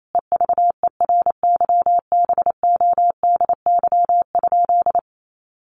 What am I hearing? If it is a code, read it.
Morse code, heard as a keyed tone, about 28 wpm, 700 Hz. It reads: EVERYBODY?
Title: Sentences from Top 1400 Words 28wpm